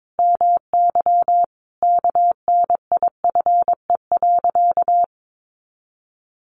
Morse code, heard as a keyed tone, about 22 wpm, 700 Hz. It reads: MY KNIFE.
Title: A Sound Of Thunder 22wpm